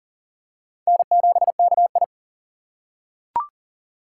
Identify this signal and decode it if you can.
Morse code, heard as a keyed tone, about 40 wpm, 700 Hz. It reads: N7KI
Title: USA Call Signs - 4 Characters - 3x Character Spacing - 40wpm